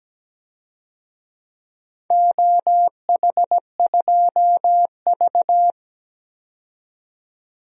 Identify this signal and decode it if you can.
Morse code, heard as a keyed tone, about 17 wpm, 700 Hz. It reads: OH2V
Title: DXCC-100 Call Signs - 4 Characters - 17wpm